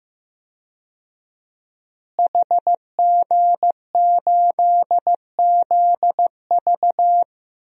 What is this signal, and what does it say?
Morse code, heard as a keyed tone, about 15 wpm, 700 Hz. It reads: HG8ZV
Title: DXCC-100 Call Signs - 5 Characters - Repeated 3x - 15wpm